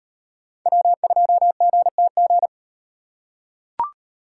Morse code, heard as a keyed tone, about 35 wpm, 700 Hz. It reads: W2GTG